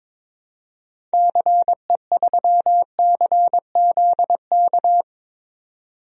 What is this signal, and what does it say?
Morse code, heard as a keyed tone, about 22 wpm, 700 Hz. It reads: CE3CZK
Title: DXCC-100 Call Signs - 6 Characters - 22wpm